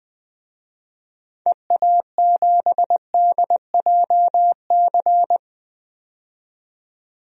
Morse code, heard as a keyed tone, about 20 wpm, 700 Hz. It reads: EA7DJC
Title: DXCC-100 Call Signs - 6 Characters - 20wpm